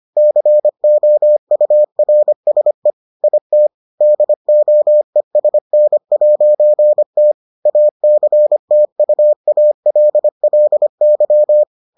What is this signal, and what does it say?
Morse code, heard as a keyed tone, about 25 wpm, 600 Hz. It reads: COURSE IT DOESN'T ACTUALLY